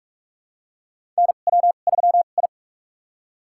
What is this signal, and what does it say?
Morse code, heard as a keyed tone, about 45 wpm, spaced as for 32 wpm, 700 Hz. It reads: NW3I
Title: USA Call Signs - 4 Characters - 2x Character Spacing - 45wpm